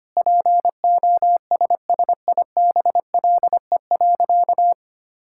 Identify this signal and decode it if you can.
Morse code, heard as a keyed tone, about 25 wpm, 700 Hz. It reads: POSSIBLE.
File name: practice / 25wpm / Sentences with 3-Word Phrases - 25wpm